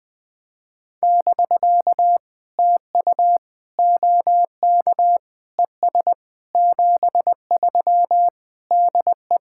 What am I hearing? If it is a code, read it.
Morse code, heard as a keyed tone, about 20 wpm, 700 Hz. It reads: <BK> TU OK ES 73 DE
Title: POTA QSO Practice - 20wpm